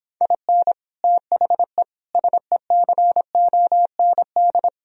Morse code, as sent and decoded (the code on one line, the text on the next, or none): .. -. / - .... / ... -.-. --- -. -..
IN THE SECOND